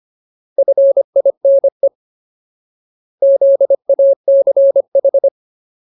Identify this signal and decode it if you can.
Morse code, heard as a keyed tone, about 25 wpm, 550 Hz. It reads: FINE ZACH
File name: Ultimate Word List / 25WPM Ultimate